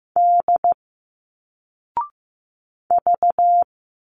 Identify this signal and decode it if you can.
Morse code, heard as a keyed tone, about 15 wpm, 700 Hz. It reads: D V
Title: Single Letters 15wpm